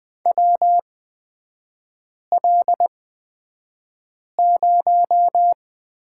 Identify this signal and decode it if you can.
Morse code, heard as a keyed tone, about 20 wpm, spaced as for 5 wpm, 700 Hz. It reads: WL0
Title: 224 Lesson 36e - BT - Three Characters 20wpm